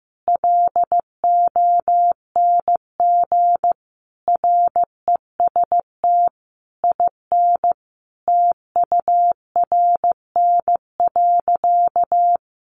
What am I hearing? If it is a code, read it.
Morse code, heard as a keyed tone, about 14 wpm, 700 Hz. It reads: LONG REST IN TURN.
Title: Sentences from Top 700 Words 15wpm